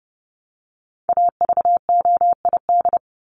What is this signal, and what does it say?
Morse code, heard as a keyed tone, about 30 wpm, 700 Hz. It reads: AVOID
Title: Top 751-1000 Words 30wpm